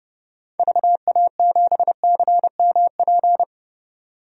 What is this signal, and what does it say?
Morse code, heard as a keyed tone, about 32 wpm, 700 Hz. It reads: VA7CMP
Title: DXCC-100 Call Signs - 6 Characters - 30wpm